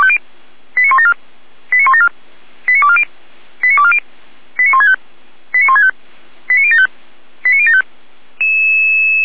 Five-tone selective call (ZVEI), heard as quick runs of five tones, tones of about 70 ms. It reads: 89174 89174 89240 89240 89165 89165 89075 89075